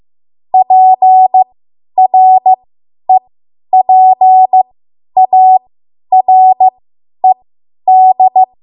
Morse code, heard as a keyed tone, about 15 wpm, spaced as for 10 wpm, 750 Hz. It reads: PREPARED
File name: Morse / 10WPM